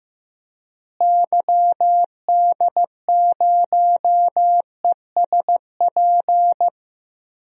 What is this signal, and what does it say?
Morse code, heard as a keyed tone, about 16 wpm, 700 Hz. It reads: YD0ESP
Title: DXCC-100 Call Signs - 6 Characters - Repeated 2x - 15wpm